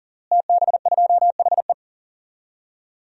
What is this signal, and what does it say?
Morse code, heard as a keyed tone, about 40 wpm, 700 Hz. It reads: TB2HE
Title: DXCC-100 Call Signs - 5 Characters - Repeated 3x - 40wpm